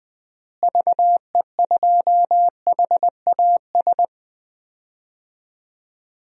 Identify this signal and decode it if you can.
Morse code, heard as a keyed tone, about 20 wpm, 700 Hz. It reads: VE2HAS
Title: DXCC-100 Call Signs - 6 Characters - Repeated 3x - 20wpm